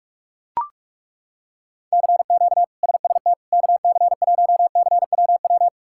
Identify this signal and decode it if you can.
Morse code, heard as a keyed tone, about 41 wpm, 700 Hz. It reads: CQ SST KC1CWW
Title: K1USN SST QSO Practice - 45wpm